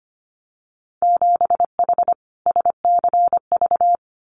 Morse code, heard as a keyed tone, about 25 wpm, 700 Hz. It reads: 7H SCV
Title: Winter Field Day Exchanges Encore 25wpm